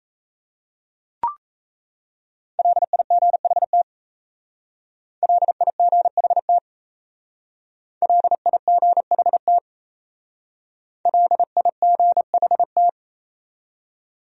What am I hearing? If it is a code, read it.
Morse code, heard as a keyed tone, about 35 wpm, 700 Hz. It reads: LIGHT LIGHT LIGHT LIGHT
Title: Speed-Racing Top 300 Words 28wpm